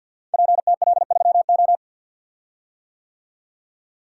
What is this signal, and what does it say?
Morse code, heard as a keyed tone, about 50 wpm, 700 Hz. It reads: PTL3O
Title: Code Groups - 5 Letter:Numbers - 50wpm